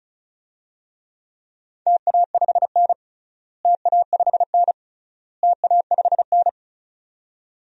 Morse code, heard as a keyed tone, about 34 wpm, 700 Hz. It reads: TA5N TA5N TA5N
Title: DXCC-100 Call Signs - 4 Characters - Repeated 3x - 35wpm